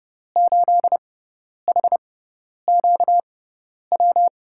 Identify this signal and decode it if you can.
Morse code, heard as a keyed tone, about 30 wpm, spaced as for 10 wpm, 700 Hz. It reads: 8HQW
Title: USA Call Signs - 6 Characters - 6x Character Spacing - 30wpm